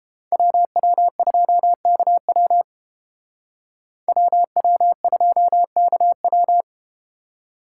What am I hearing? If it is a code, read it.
Morse code, heard as a keyed tone, about 31 wpm, 700 Hz. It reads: WW2KW WW2KW